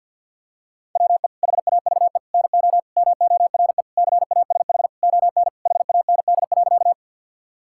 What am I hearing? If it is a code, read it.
Morse code, heard as a keyed tone, about 52 wpm, 700 Hz. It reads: WE HAVE NO MORE CASH ON HAND.
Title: Sentences from Top 1000 Words - Review Part 5 - 50wpm